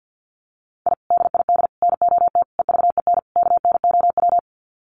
Morse code, heard as a keyed tone, about 50 wpm, 700 Hz. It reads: I DID NOT EVEN KNOW
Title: Sets of 5 Words Encore 50wpm